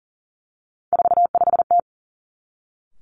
Morse code, heard as a keyed tone, about 42 wpm, 700 Hz. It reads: 45T
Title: Three Challenging Letter-Number Combinations 40wpm